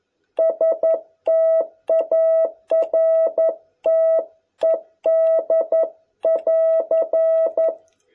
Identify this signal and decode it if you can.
Morse code, heard as a keyed tone, about 11 wpm, 660 Hz. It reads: STARTED <AR>